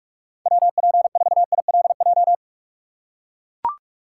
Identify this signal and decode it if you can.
Morse code, heard as a keyed tone, about 42 wpm, 700 Hz. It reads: WP4ILJ
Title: USA Call Signs - 6x Character Spacing - 45wpm